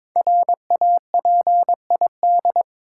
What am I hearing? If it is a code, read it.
Morse code, heard as a keyed tone, about 23 wpm, 700 Hz. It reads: RAPID
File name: 5 Letter Words - First 3 Letters Given - 22wpm